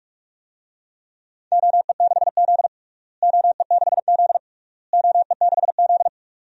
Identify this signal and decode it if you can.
Morse code, heard as a keyed tone, about 46 wpm, 700 Hz. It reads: OE6Z OE6Z OE6Z
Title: DXCC-100 Call Signs - 4 Characters - Repeated 3x - 45wpm